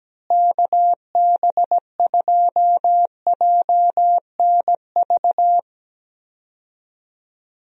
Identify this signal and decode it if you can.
Morse code, heard as a keyed tone, about 17 wpm, 700 Hz. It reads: KB2JNV
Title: USA Call Signs - 6 Characters - Repeated 3x - 17wpm